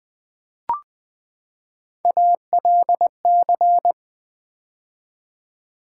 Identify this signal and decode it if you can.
Morse code, heard as a keyed tone, about 20 wpm, 700 Hz. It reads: ALC
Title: Build A Word - 3 Letters At A Time - 9-Letter Words - 20wpm